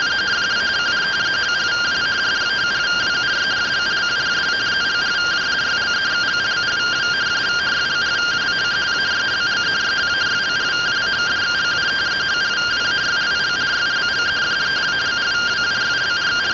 Two-tone teleprinter signal, about 170 Hz shift.